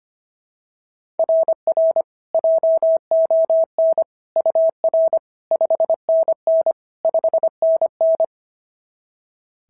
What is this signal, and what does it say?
Morse code, heard as a keyed tone, about 22 wpm, 650 Hz. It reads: RR JON UR 5NN 5NN
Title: SOTA QSO Practice 25wpm